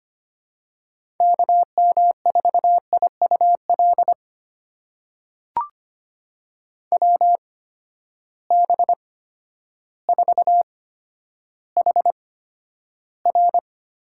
Morse code, heard as a keyed tone, about 25 wpm, spaced as for 11 wpm, 700 Hz. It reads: KM4IUL WB4HR
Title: USA Call Signs - 6 Characters - 8x Character Spacing - 25wpm